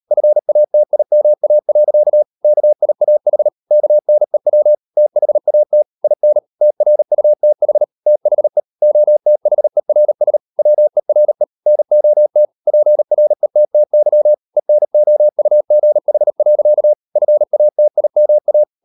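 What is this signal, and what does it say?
Morse code, heard as a keyed tone, about 38 wpm, 600 Hz. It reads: FATIMA. KIAH KNEW THAT IN TRUTH THE OTHERS WERE NOT PRETTY ENOUGH. FATIMA
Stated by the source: Morse code practice at 38 WPM, sending Ken Follett's novel Never in English, from CW med Gnister.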